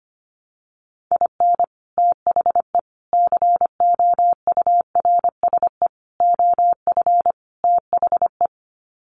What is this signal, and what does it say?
Morse code, heard as a keyed tone, about 25 wpm, 700 Hz. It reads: IN THE COURSE OF THE
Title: Sets of 5 Words 25wpm